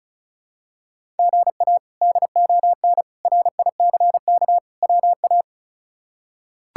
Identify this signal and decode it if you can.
Morse code, heard as a keyed tone, about 36 wpm, 700 Hz. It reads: GA DON RICK WA
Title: K1USN SST QSO Practice - 35wpm